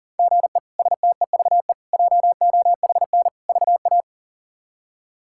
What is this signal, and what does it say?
Morse code, heard as a keyed tone, about 39 wpm, 700 Hz. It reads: GE STEVE JOHN VA
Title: K1USN SST QSO Practice - 40wpm